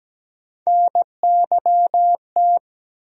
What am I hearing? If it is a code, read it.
Morse code, heard as a keyed tone, about 17 wpm, 700 Hz. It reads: NYT
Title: USA Call Sign Postfixes - 3 Letters 17wpm